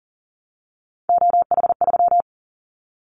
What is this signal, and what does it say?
Morse code, heard as a keyed tone, about 40 wpm, 700 Hz. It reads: OH3
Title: Callsign Prefixes for Top 25 DXCC - 40wpm